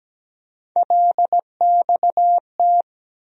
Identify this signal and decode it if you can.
Morse code, heard as a keyed tone, about 17 wpm, 700 Hz. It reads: LXT
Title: USA Call Sign Postfixes - 3 Letters 17wpm